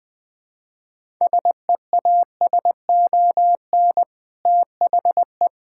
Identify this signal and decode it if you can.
Morse code, heard as a keyed tone, about 20 wpm, 700 Hz. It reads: SEASON THE
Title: Sentences with 5-Word Phrases - Spoken Beginning - 20wpm